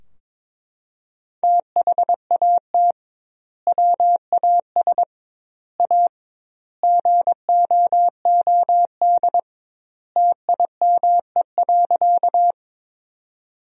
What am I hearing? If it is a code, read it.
Morse code, heard as a keyed tone, about 22 wpm, 700 Hz. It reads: THAT WAS A GOOD TIME.
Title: 2x Word-Spacing - Sentences from Top 200 Words - 22wpm